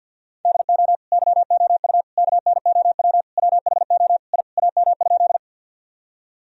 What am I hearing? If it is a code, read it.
Morse code, heard as a keyed tone, about 50 wpm, 700 Hz. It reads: DO YOU KNOW WHO I AM?
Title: Sentences from Top 1000 Words - Review Part 2 - 50wpm